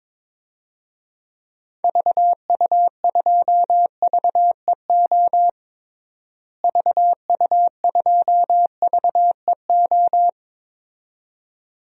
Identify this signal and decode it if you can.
Morse code, heard as a keyed tone, about 22 wpm, 700 Hz. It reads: VU2VEO VU2VEO
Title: DXCC-100 Call Signs - 6 Characters - Repeated 2x - 22wpm